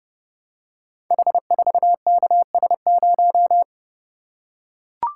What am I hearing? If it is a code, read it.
Morse code, heard as a keyed tone, about 30 wpm, 700 Hz. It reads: H4KS0
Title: Code Groups - 5 Letter:Numbers - 30wpm